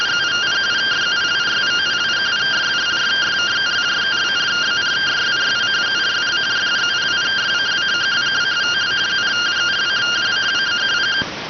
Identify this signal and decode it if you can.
Two-tone teleprinter signal, about 170 Hz shift.